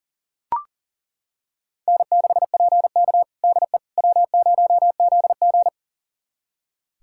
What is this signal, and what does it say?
Morse code, heard as a keyed tone, about 40 wpm, 700 Hz. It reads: N6PK DE W0ZG
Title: USA Callsign DE Callsign - 4 Characters - 40wpm